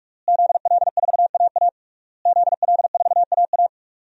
Morse code, heard as a keyed tone, about 46 wpm, 700 Hz. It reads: ZL4AA ZL4AA
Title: DXCC-100 Call Signs - 5 Characters - Repeated 3x - 45wpm